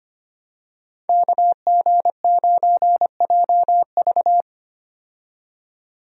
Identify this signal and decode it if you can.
Morse code, heard as a keyed tone, about 25 wpm, 700 Hz. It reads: KG9JV
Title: USA Call Signs - 5 Characters - 25wpm